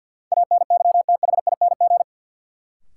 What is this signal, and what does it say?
Morse code, heard as a keyed tone, about 53 wpm, 700 Hz. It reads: ANYTHING
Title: Sentences from Top 1000 Words - New Word Drill - 50wpm